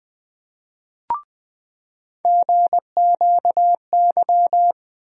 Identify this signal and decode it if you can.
Morse code, heard as a keyed tone, about 20 wpm, 700 Hz. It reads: GQY
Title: USA Call Sign Postfixes - 3 Letters 20wpm